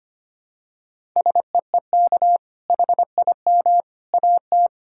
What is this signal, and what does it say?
Morse code, heard as a keyed tone, about 25 wpm, 700 Hz. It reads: SEEK HIM AT
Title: Sentences from Top 1600 Words 25wpm